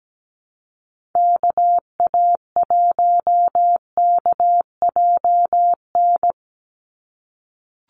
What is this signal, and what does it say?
Morse code, heard as a keyed tone, about 17 wpm, 700 Hz. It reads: KA1KJN
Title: USA Call Signs 17wpm